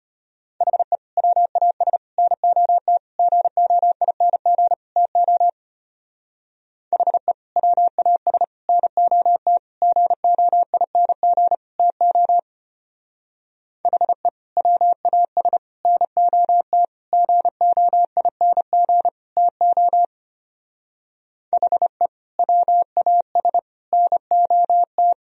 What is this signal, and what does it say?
Morse code, heard as a keyed tone, about 33 wpm, 700 Hz. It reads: HE WAS NOT GOING TO HE WAS NOT GOING TO HE WAS NOT GOING TO HE WAS NOT